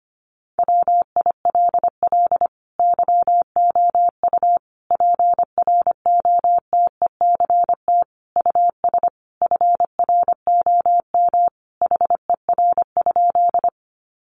Morse code, heard as a keyed tone, about 25 wpm, 700 Hz. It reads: WILL YOU PROTECT US FROM HER?
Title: Sentences from Top 900 Words 25wpm